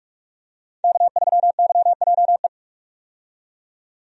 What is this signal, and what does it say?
Morse code, heard as a keyed tone, about 45 wpm, 700 Hz. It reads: K3YJE
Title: USA Call Signs - 5 Characters - 45wpm